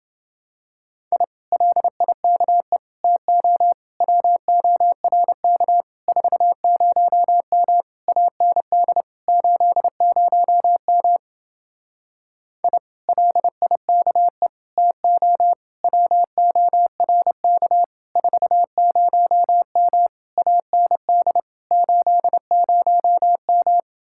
Morse code, heard as a keyed tone, about 28 wpm, 700 Hz. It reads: I LIKE TO WORK 40M AND 80M I LIKE TO WORK 40M AND 80M